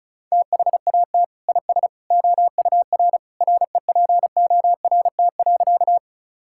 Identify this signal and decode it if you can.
Morse code, heard as a keyed tone, about 34 wpm, 700 Hz. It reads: THAT IS OUR REPORT.
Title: Sentences from Top 1000 Words - Review Part 6 - 35wpm